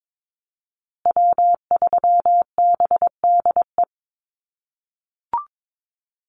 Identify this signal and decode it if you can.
Morse code, heard as a keyed tone, about 22 wpm, 700 Hz. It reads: W3BDE
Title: USA Call Signs Repeated 2x 22wpm